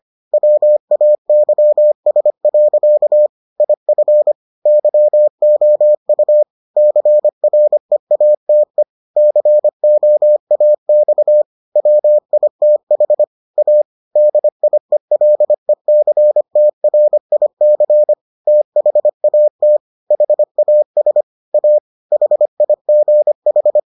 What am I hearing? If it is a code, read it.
Morse code, heard as a keyed tone, about 25 wpm, 600 Hz. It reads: WAYS. IF YOU CREATE COAX WITH A DIELECTRIC THAT HAS A HIGH